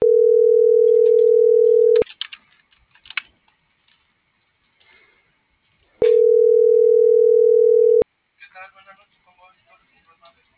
Ringback tone (North America), 440 + 480 Hz.